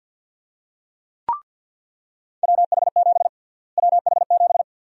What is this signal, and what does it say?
Morse code, heard as a keyed tone, about 50 wpm, 700 Hz. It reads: WH7 WH7
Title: USA Call Sign Prefixes - Repeated 2x - 50wpm